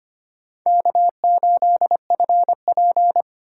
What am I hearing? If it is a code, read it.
Morse code, heard as a keyed tone, about 25 wpm, 700 Hz. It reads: K8FP
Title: USA Call Signs - 2x Character Spacing - 25wpm